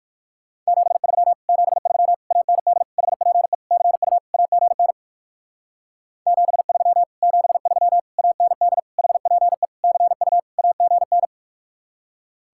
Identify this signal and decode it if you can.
Morse code, heard as a keyed tone, about 55 wpm, 700 Hz. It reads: 73 73 AND HPE CU AGN 73 73 AND HPE CU AGN
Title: Speed-Racing Common QSO Phrases - 45wpm